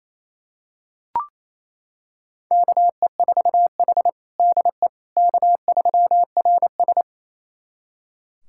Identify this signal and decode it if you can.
Morse code, heard as a keyed tone, about 28 wpm, 690 Hz. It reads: KE4H DE K3RS
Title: USA Callsign DE Callsign - 4 Characters - 28wpm